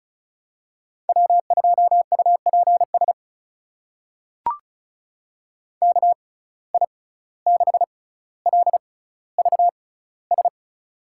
Morse code, heard as a keyed tone, about 34 wpm, spaced as for 17 wpm, 700 Hz. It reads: W2UPS KI6LVS
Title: USA Call Signs - 6x Character Spacing - 35wpm